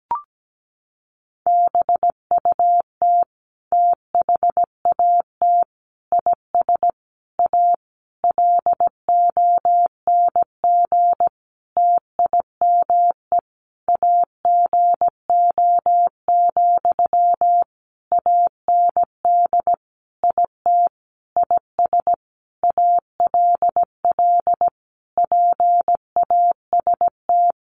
Morse code, heard as a keyed tone, about 17 wpm, 700 Hz. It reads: BUT THAT IS A LONG TIME AGO, AND IT IS ALL PAST
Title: Sentences from Top 600 Words 17wpm